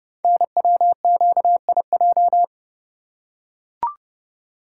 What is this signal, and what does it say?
Morse code, heard as a keyed tone, about 30 wpm, 700 Hz. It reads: NWQIJ
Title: Code Groups - 5 Letter:Numbers - 30wpm